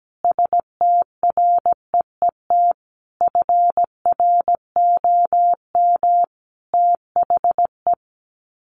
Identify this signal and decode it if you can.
Morse code, heard as a keyed tone, about 17 wpm, 700 Hz. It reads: STREET FROM THE
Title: Sets of 5 Words Encore 17wpm